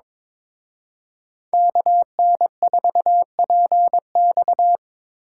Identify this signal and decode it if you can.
Morse code, heard as a keyed tone, about 25 wpm, 700 Hz. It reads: KN4PX
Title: Speed Racing - USA Call Signs 22wpm